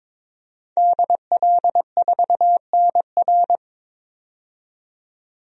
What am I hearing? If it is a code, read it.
Morse code, heard as a keyed tone, about 22 wpm, 700 Hz. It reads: DL4NR
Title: DXCC-100 Call Signs - 5 Characters - 22wpm